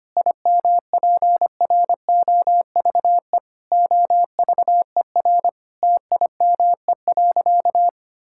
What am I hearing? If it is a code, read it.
Morse code, heard as a keyed tone, about 25 wpm, 700 Hz. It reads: IMPROVE OVER TIME.
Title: Sentences from Top 1000 Words - Review Part 5 - 25wpm